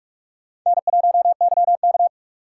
Morse code, heard as a keyed tone, about 46 wpm, 700 Hz. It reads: N1YK
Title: USA Call Signs - 4x Character Spacing - 45wpm